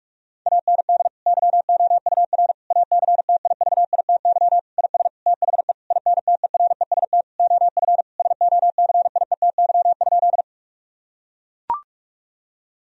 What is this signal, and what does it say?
Morse code, heard as a keyed tone, about 41 wpm, 700 Hz. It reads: AND YOUR ACTIVITY IS THE INTEREST OF SOCIETY?
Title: Sentences from Top 1000 Words - Review Part 4 - 45wpm